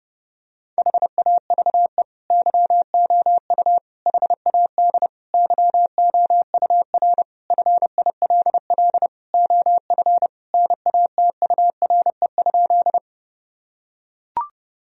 Morse code, heard as a keyed tone, about 29 wpm, 700 Hz. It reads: HAVE YOU HAD YOUR FILL OF NATURE?
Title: Sentences from Top 1000 Words - Review Part 3 - 30wpm